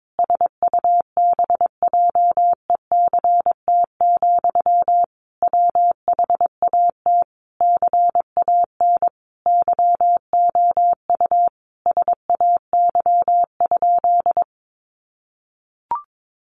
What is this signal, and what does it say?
Morse code, heard as a keyed tone, about 22 wpm, 700 Hz. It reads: SUBJECT, WHAT CAN YOU SAY?
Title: Sentences from Top 800 Words 22wpm